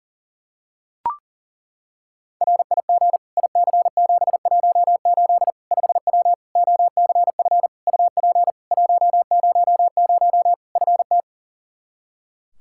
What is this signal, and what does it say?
Morse code, heard as a keyed tone, about 40 wpm, 700 Hz. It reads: RIG IC718 5W OCF UP 100 FT
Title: K1USN SST QSO Practice 40wpm